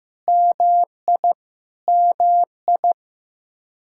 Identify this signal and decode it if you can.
Morse code, heard as a keyed tone, about 15 wpm, 700 Hz. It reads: MI MI
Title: Practice QSO - Exchange 1 - Beginning - 15wpm